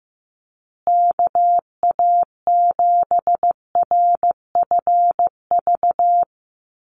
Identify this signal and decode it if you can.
Morse code, heard as a keyed tone, about 15 wpm, 700 Hz. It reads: KA7RFV
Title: USA Call Signs Repeated 2x 15wpm